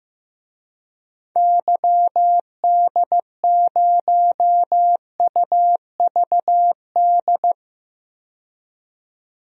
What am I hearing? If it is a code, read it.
Morse code, heard as a keyed tone, about 15 wpm, 700 Hz. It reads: YD0UVD
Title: DXCC-100 Call Signs - 6 Characters - 15wpm